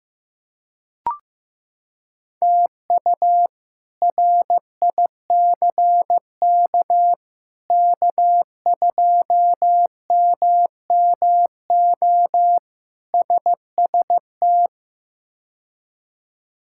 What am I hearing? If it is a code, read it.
Morse code, heard as a keyed tone, about 15 wpm, 700 Hz. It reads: TU RICK K2MMO SST
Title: K1USN SST QSO Practice - 15wpm